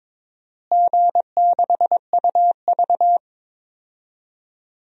Morse code, heard as a keyed tone, about 22 wpm, 700 Hz. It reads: G6UV
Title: DXCC-100 Call Signs - 4 Characters - Repeated 2x - 22wpm